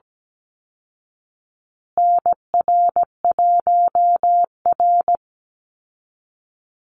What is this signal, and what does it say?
Morse code, heard as a keyed tone, about 17 wpm, 700 Hz. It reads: NR1R
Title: Call Signs Repeated 3x - 17wpm